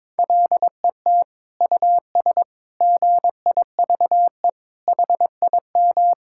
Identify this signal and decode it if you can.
Morse code, heard as a keyed tone, about 22 wpm, 700 Hz. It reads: LET US GIVE HIM
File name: Sentences with 7-Word Phrases - 22wpm